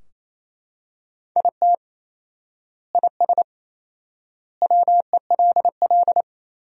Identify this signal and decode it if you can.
Morse code, heard as a keyed tone, about 28 wpm, 700 Hz. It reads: IT IS WELL
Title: 4x Word-Spacing - Sentences from Top 300 Words - 28wpm